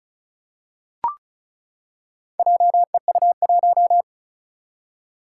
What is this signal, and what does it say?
Morse code, heard as a keyed tone, about 36 wpm, 700 Hz. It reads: JEU1
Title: Four Challenging Letter-Number Combinations 35wpm